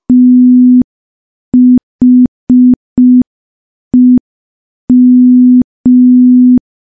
Morse code, heard as a keyed tone, about 5 wpm, 250 Hz. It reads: THEM
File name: morse / morse-training-05WPM